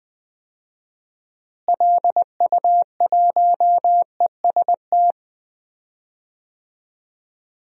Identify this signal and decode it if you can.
Morse code, heard as a keyed tone, about 20 wpm, 700 Hz. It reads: LU1EST
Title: DXCC-100 Call Signs - 6 Characters - Repeated 3x - 20wpm